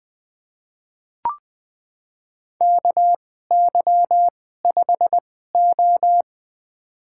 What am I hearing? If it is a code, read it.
Morse code, heard as a keyed tone, about 20 wpm, spaced as for 14 wpm, 700 Hz. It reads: KY5O
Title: USA Call Signs - 4 Characters - 2x Character Spacing - 20wpm